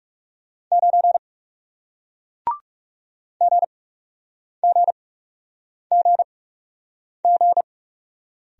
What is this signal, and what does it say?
Morse code, heard as a keyed tone, about 41 wpm, spaced as for 8 wpm, 700 Hz. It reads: 9 GGGG